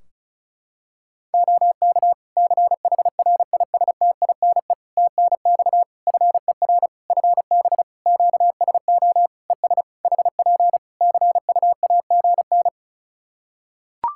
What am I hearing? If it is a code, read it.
Morse code, heard as a keyed tone, about 36 wpm, 700 Hz. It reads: OK CHRISTINE TNX FER FB QSO ES HP CUAGN
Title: Practice QSO - Exchange 4 - Ending - 35wpm